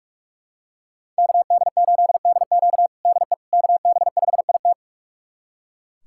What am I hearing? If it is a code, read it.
Morse code, heard as a keyed tone, about 43 wpm, 700 Hz. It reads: KD8DQ DE KB5IT
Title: USA Callsign DE Callsign - 5 Characters - 45wpm